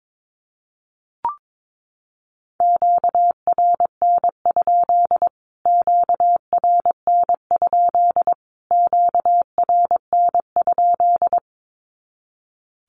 Morse code, heard as a keyed tone, about 22 wpm, 700 Hz. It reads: QRN? QRN? QRN?
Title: Q-Codes Repeated 3x 22wpm